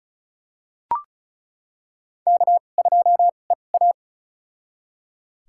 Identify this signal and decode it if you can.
Morse code, heard as a keyed tone, about 34 wpm, spaced as for 25 wpm, 700 Hz. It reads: K2EA
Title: USA Call Signs - 4 Characters - 2x Character Spacing - 35wpm